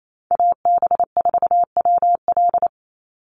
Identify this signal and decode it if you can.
Morse code, heard as a keyed tone, about 28 wpm, 700 Hz. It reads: AB4WL
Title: NA SOTA Chaser Callsigns 28wpm